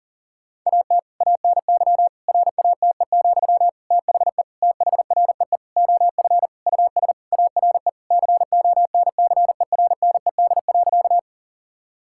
Morse code, heard as a keyed tone, about 37 wpm, 700 Hz. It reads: AT ANY RATE, THE THREE OF US ARE CONCERNED.
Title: Sentences from Top 1000 Words - Review Part 6 - 40wpm